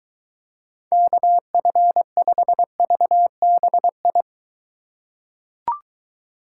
Morse code, heard as a keyed tone, about 23 wpm, 700 Hz. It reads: KF5VBI